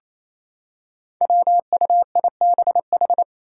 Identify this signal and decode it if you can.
Morse code, heard as a keyed tone, about 28 wpm, 700 Hz. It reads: WUIBH
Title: Five Challenging Letter-Number Combinations 28wpm